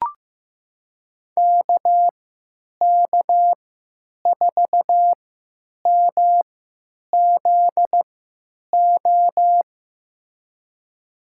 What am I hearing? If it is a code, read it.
Morse code, heard as a keyed tone, about 15 wpm, spaced as for 9 wpm, 700 Hz. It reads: KK4MZO
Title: USA Call Signs - 6 Characters - 3x Character Spacing - 15wpm